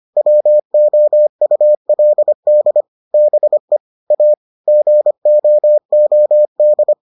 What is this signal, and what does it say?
Morse code, heard as a keyed tone, about 25 wpm, 600 Hz. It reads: WOULD BE A GOOD